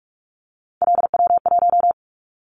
Morse code, heard as a keyed tone, about 46 wpm, 700 Hz. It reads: LW1
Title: Callsign Prefixes for Top 50 DXCC - 45wpm